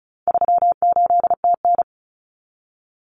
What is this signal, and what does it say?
Morse code, heard as a keyed tone, about 34 wpm, 700 Hz. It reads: 38TN
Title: Common Cut Numbers - 4-Digit 35wpm